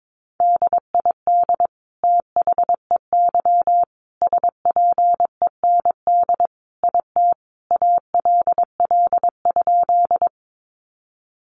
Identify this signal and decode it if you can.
Morse code, heard as a keyed tone, about 22 wpm, 700 Hz. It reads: DID THEY SPEND IT ALL?
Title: Sentences from Top 800 Words 22wpm